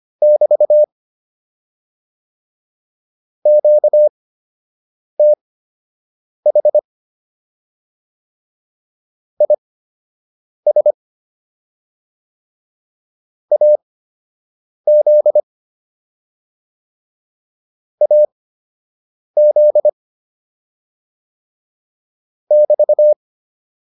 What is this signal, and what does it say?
Morse code, heard as a keyed tone, about 25 wpm, spaced as for 7 wpm, 600 Hz. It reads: <BT> QTH IS AZ AZ <BT>
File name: QSO 103 07